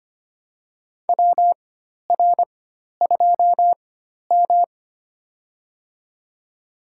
Morse code, heard as a keyed tone, about 25 wpm, spaced as for 12 wpm, 700 Hz. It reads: WR2M
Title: USA Call Signs - 4 Characters - 4x Character Spacing - 25wpm